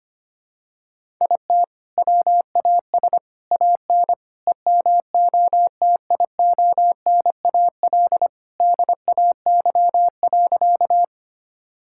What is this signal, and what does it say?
Morse code, heard as a keyed tone, about 25 wpm, 700 Hz. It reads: IT WAS AN EMOTIONAL DAY.
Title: Sentences from Top 1000 Words - Review Part 5 - 25wpm